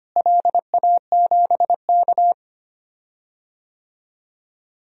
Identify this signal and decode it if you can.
Morse code, heard as a keyed tone, about 25 wpm, 700 Hz. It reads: LA7K
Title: DXCC-100 Call Signs - 4 Characters - Repeated 3x - 25wpm